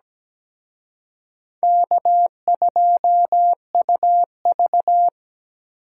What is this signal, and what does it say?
Morse code, heard as a keyed tone, about 20 wpm, 700 Hz. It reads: K2UV
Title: Speed Racing - USA Call Signs 17wpm